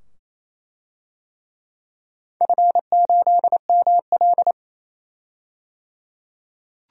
Morse code, heard as a keyed tone, about 28 wpm, 700 Hz. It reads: F8ML
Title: DXCC-100 Call Signs - 4 Characters - 28wpm